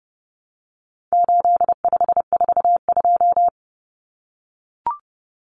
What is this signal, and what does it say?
Morse code, heard as a keyed tone, about 30 wpm, 700 Hz. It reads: 8542
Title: Common Cut Numbers - 4-Digit 30wpm